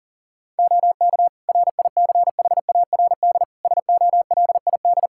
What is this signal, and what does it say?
Morse code, heard as a keyed tone, about 40 wpm, 700 Hz. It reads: OK RICHARD SOLID
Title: Practice QSO - Exchange 3 - More Info - 40wpm